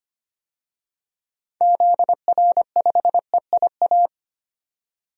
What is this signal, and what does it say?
Morse code, heard as a keyed tone, about 25 wpm, 700 Hz. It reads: ZR5EIA
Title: DXCC-100 Call Signs - 6 Characters - 25wpm